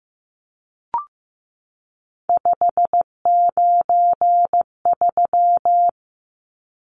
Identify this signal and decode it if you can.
Morse code, heard as a keyed tone, about 15 wpm, 700 Hz. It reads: 593
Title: Numbers - 3-Digit 15wpm